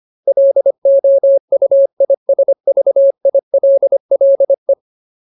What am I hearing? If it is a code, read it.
Morse code, heard as a keyed tone, about 25 wpm, 550 Hz. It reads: LOUISVILLE
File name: Ultimate Word List / 25WPM Ultimate